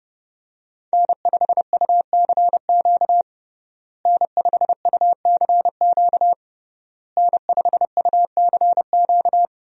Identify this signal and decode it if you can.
Morse code, heard as a keyed tone, about 30 wpm, 700 Hz. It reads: N5UCQ N5UCQ N5UCQ
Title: USA Call Signs - 5 Characters - Repeated 3x - 30wpm